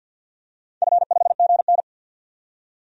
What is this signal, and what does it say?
Morse code, heard as a keyed tone, about 51 wpm, 700 Hz. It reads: F5GN
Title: DXCC-100 Call Signs - 4 Characters - 50wpm